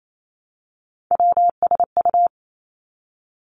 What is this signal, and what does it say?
Morse code, heard as a keyed tone, about 28 wpm, 700 Hz. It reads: WSU
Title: Three Challenging Letter-Number Combinations 28wpm